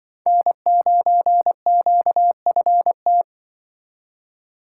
Morse code, heard as a keyed tone, about 25 wpm, 700 Hz. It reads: N9QFT
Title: Speed Racing - USA 1x3 Call Signs 20wpm